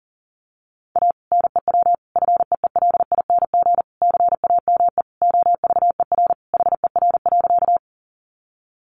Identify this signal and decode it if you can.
Morse code, heard as a keyed tone, about 40 wpm, 700 Hz. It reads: A NEW FEELING CAME OVER HER.
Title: Sentences from Top 1200 Words 40wpm